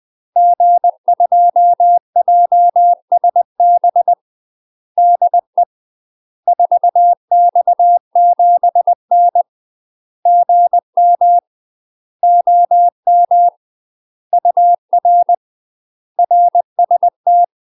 Morse code, heard as a keyed tone, about 20 wpm, 700 Hz. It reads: G2JSB DE 4X7N GM OM UR RST
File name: UK MASTER QSOs TRAINING / 20WPM / UK MASTER QSOs 0036